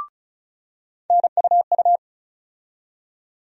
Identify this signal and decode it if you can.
Morse code, heard as a keyed tone, about 35 wpm, 700 Hz. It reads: NUU
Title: USA Call Sign Postfixes - 3 Letters 35wpm